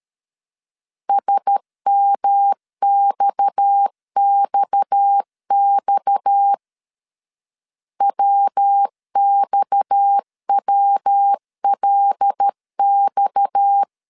Morse code, heard as a keyed tone, about 13 wpm, 800 Hz. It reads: SMXXX WXWLX
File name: LEZIONI / 6 PKX0 / 6 PKX 12wpm 2024